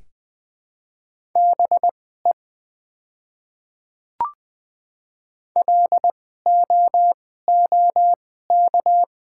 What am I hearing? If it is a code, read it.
Morse code, heard as a keyed tone, about 20 wpm, spaced as for 15 wpm, 690 Hz. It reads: BE LOOK
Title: 2x Character Spacing - Top 100 Words - 20wpm